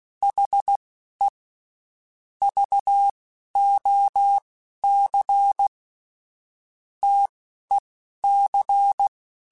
Morse code, heard as a keyed tone, about 16 wpm, spaced as for 11 wpm, 800 Hz. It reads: HE VOC TEC
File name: QSOs / QSOs-mp3s / WD0BGP 1@15